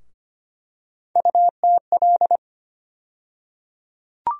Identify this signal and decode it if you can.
Morse code, heard as a keyed tone, about 25 wpm, 700 Hz. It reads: UTL
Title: USA Call Sign Postfixes - 3 Letters 25wpm